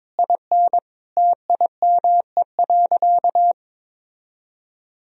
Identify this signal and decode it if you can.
Morse code, heard as a keyed tone, about 22 wpm, 700 Hz. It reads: IN TIME.
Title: Sentences from Top 1000 Words - Review Part 3 - 22wpm